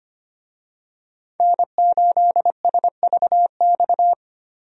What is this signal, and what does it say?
Morse code, heard as a keyed tone, about 25 wpm, 700 Hz. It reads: N8SVX
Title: USA Call Signs - 5 Characters - Repeated 2x - 25wpm